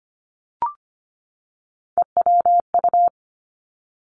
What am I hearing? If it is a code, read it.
Morse code, heard as a keyed tone, about 25 wpm, 700 Hz. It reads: EWU
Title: Three Challenging Letter-Number Combinations 25wpm